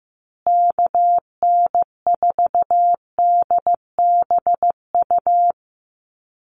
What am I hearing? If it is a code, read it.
Morse code, heard as a keyed tone, about 15 wpm, 700 Hz. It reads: KN4DBU
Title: USA Call Signs Repeated 3x 15wpm